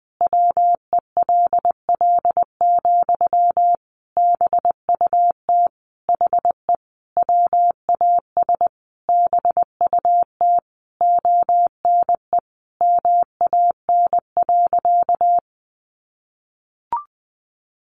Morse code, heard as a keyed tone, about 20 wpm, 700 Hz. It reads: WELL, BUT HE WAS BUT ONE MAN.
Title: Sentences from Top 1500 Words 20wpm